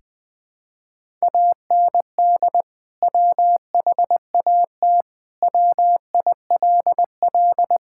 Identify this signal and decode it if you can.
Morse code, heard as a keyed tone, about 21 wpm, 700 Hz. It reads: AND WHAT WILL
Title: Sentences with 2-Word Phrases - Spoken Beginning - 20wpm